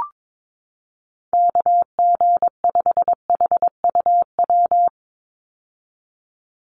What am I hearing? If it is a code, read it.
Morse code, heard as a keyed tone, about 22 wpm, 700 Hz. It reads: KG5HUW
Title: USA Call Signs 22wpm